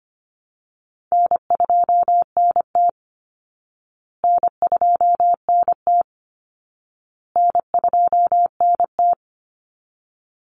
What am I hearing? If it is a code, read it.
Morse code, heard as a keyed tone, about 25 wpm, 700 Hz. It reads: N2NT N2NT N2NT
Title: Call Signs Repeated 2x - 25wpm